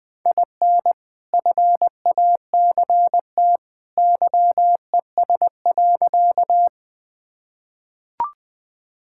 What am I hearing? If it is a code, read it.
Morse code, heard as a keyed tone, about 20 wpm, 700 Hz. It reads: IN FACT YES.
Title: Sentences with 2-Word Phrases - 20wpm